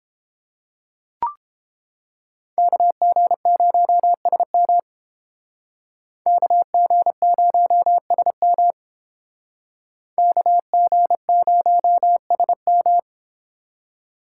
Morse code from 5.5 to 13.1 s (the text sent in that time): KG0SM KG0SM